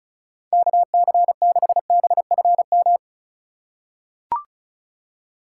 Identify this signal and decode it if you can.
Morse code, heard as a keyed tone, about 36 wpm, 700 Hz. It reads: KC6BFM
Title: USA Call Signs - 6 Characters - 4x Character Spacing - 35wpm